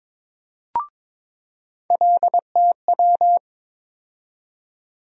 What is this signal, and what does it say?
Morse code, heard as a keyed tone, about 22 wpm, 700 Hz. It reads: LTW
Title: USA Call Sign Postfixes - 3 Letters 22wpm